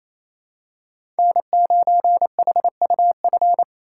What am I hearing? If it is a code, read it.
Morse code, heard as a keyed tone, about 28 wpm, 700 Hz. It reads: N9HUF
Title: USA Call Signs - 3x Character Spacing - 28wpm